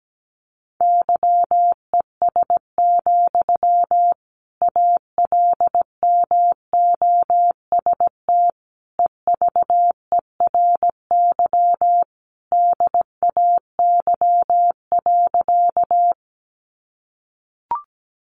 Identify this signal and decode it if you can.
Morse code, heard as a keyed tone, about 17 wpm, 700 Hz. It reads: YES, ALMOST EVERY DAY.
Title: Sentences from Top 700 Words 17wpm